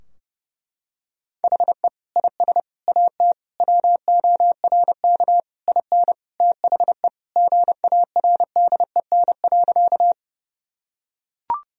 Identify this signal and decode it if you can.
Morse code, heard as a keyed tone, about 32 wpm, 700 Hz. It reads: HE IS AT WORK IN THE GARDEN.
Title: Sentences from Top 1000 Words - Review Part 4 - 30wpm